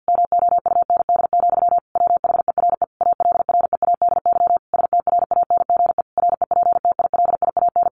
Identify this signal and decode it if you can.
Morse code, heard as a keyed tone, about 50 wpm, 700 Hz. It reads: MOUND, WHERE ALREADY STRANGE REPTILIAN
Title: A Sound Of Thunder 50wpm